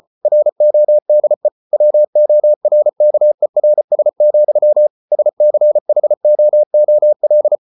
Morse code, heard as a keyed tone, about 34 wpm, 600 Hz. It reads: RODE WORKERS, SCHOOL